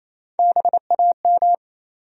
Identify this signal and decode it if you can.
Morse code, heard as a keyed tone, about 27 wpm, 700 Hz. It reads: BAM
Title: USA Call Sign Postfixes - 3 Letters 28wpm